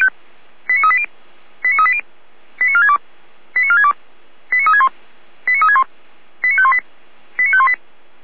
Five-tone selective call (ZVEI), heard as quick runs of five tones, tones of about 70 ms. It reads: 89380 89380 89462 89462 89361 89361 89518 89518
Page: /455/SDR/ZVEI/89533/2021.06.23/